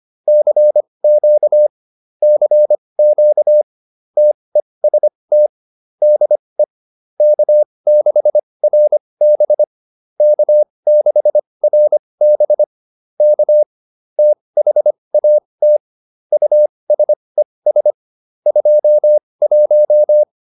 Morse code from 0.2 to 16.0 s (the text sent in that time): CQ CQ TEST DE K6RB K6RB K THAT